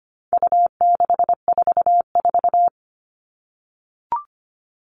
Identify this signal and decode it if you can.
Morse code, heard as a keyed tone, about 25 wpm, 700 Hz. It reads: U644
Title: All Cut Numbers - 4-Digit 25wpm